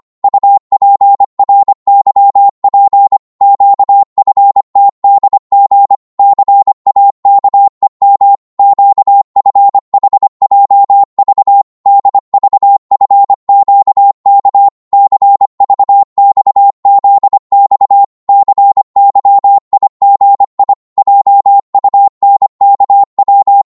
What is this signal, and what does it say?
Morse code, heard as a keyed tone, about 25 wpm, 800 Hz. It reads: UPRYP QFTDG CAKEM QFHJV DVFQK CVXZX CYIGI JUNKW
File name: Morse / ltr-25wpm